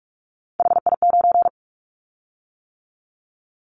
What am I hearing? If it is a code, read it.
Morse code, heard as a keyed tone, about 46 wpm, 700 Hz. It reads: HI9
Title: Callsign Prefixes for Top 100 DXCC - 45wpm